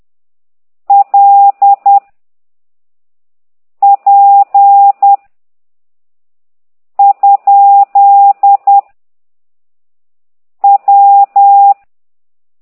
Morse code, heard as a keyed tone, about 10 wpm, 800 Hz. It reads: L P ? W
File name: ct2idw / exerc9